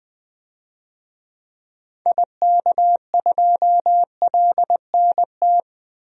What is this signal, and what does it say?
Morse code, heard as a keyed tone, about 20 wpm, 700 Hz. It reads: IK2LNT
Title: DXCC-100 Call Signs - 6 Characters - 20wpm